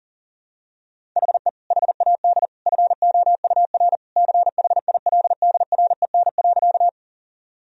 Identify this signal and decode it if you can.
Morse code, heard as a keyed tone, about 40 wpm, 700 Hz. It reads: HE HAD FOUR CHILDREN.
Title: Sentences from Top 1000 Words - Review Part 1 - 40wpm